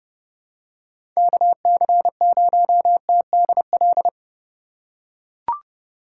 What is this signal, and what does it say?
Morse code, heard as a keyed tone, about 30 wpm, 700 Hz. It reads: KC0TDL
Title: USA Call Signs - 6 Characters - 4x Character Spacing - 30wpm